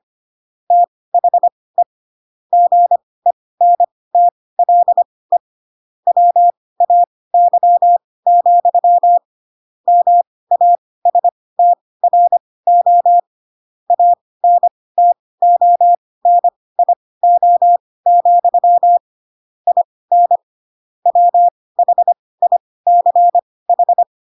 Morse code, heard as a keyed tone, about 25 wpm, spaced as for 18 wpm, 700 Hz. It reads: THE GENTLE WAY, MASTRO ANTONIO, IN WHICH
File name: Pinocchio / Pinocchio ch2 12 25 ik1hge